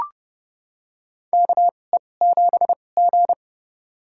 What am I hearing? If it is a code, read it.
Morse code, heard as a keyed tone, about 31 wpm, spaced as for 22 wpm, 700 Hz. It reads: KE7G
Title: USA Call Signs - 4 Characters - 2x Character Spacing - 30wpm